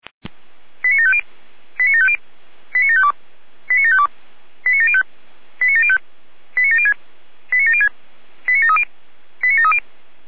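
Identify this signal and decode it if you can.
Five-tone selective call (ZVEI), heard as quick runs of five tones, tones of about 70 ms. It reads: 89750 89750 89752 89752 89705 89705 89706 89706 89730 89730
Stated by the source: Five-tone ZVEI selective call received by SDR.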